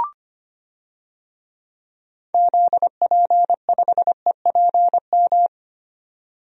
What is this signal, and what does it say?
Morse code, heard as a keyed tone, about 25 wpm, 700 Hz. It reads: ZP5EPM
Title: DXCC-100 Call Signs - 6 Characters - 25wpm